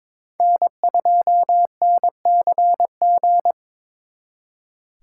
Morse code, heard as a keyed tone, about 26 wpm, 700 Hz. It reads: N2NCG
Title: Speed Racing - USA 1x3 Call Signs 22wpm